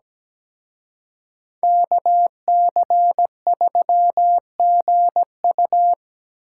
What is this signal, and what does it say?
Morse code, heard as a keyed tone, about 20 wpm, 700 Hz. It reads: KC3GU
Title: Speed Racing - USA 2x2 Call Signs 17wpm